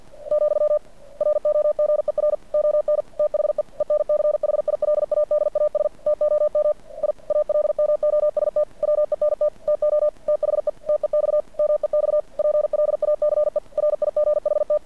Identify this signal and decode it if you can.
Morse code, heard as a keyed tone, about 50 wpm, 590 Hz. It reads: , ROGER ON THE ENCHILADAS TOM I ALMOST WENT TO THE TEX MEX PLACE RIGHT
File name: archived / CRSnet